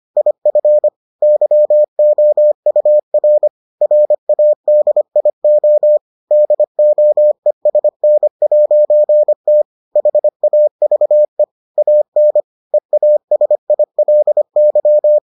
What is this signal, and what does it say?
Morse code, heard as a keyed tone, about 25 wpm, 600 Hz. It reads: IF YOUR RADIO DOESN'T HAVE AN EASILY